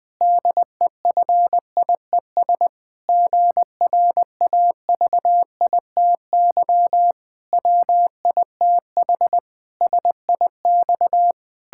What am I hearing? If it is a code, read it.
Morse code, heard as a keyed tone, about 20 wpm, 700 Hz. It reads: DEFIES GRAVITY WITH SIX
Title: Pangram Challenge v2 - 20wpm